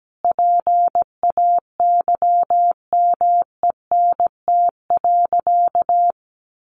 Morse code, heard as a keyed tone, about 17 wpm, 700 Hz. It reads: PAYMENT.
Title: Sentences from Top 1200 Words 17wpm